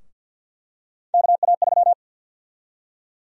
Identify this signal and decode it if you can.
Morse code, heard as a keyed tone, about 48 wpm, 700 Hz. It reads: KA3
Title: USA Call Sign Prefixes - 50wpm